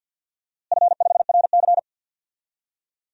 Morse code, heard as a keyed tone, about 51 wpm, 700 Hz. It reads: F5RC
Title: DXCC-100 Call Signs - 4 Characters - Repeated 2x - 50wpm